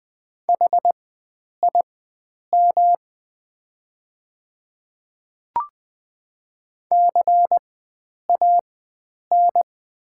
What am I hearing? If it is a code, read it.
Morse code, heard as a keyed tone, about 20 wpm, spaced as for 9 wpm, 700 Hz. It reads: HIM CAN
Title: 4x Character Spacing - Top 100 Words - 20wpm